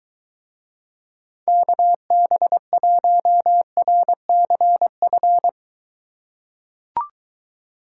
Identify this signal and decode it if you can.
Morse code, heard as a keyed tone, about 23 wpm, 700 Hz. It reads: KB1RCF